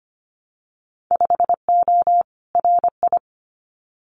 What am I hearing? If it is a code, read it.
Morse code, heard as a keyed tone, about 25 wpm, 690 Hz. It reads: 5O RI
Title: Winter Field Day Exchanges Encore 25wpm